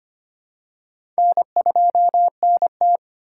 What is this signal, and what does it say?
Morse code, heard as a keyed tone, about 25 wpm, 700 Hz. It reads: N2NT
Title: USA Call Signs - 4 Characters - 4x Character Spacing - 25wpm